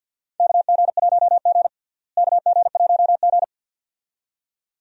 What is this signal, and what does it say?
Morse code, heard as a keyed tone, about 52 wpm, 700 Hz. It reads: KG1G KG1G
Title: USA Call Signs - 4 Characters - Repeated 2x - 50wpm